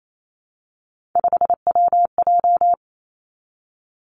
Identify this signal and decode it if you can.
Morse code, heard as a keyed tone, about 28 wpm, 700 Hz. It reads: 5WJ
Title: Three Challenging Letter-Number Combinations 28wpm